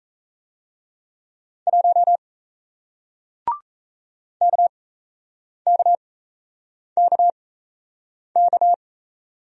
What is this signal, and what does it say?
Morse code, heard as a keyed tone, about 38 wpm, spaced as for 8 wpm, 700 Hz. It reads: 1 KKKK